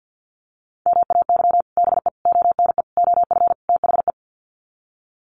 Morse code, heard as a keyed tone, about 50 wpm, 700 Hz. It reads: MAY BE ONE OF THE
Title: Sets of 5 Words Encore2 50wpm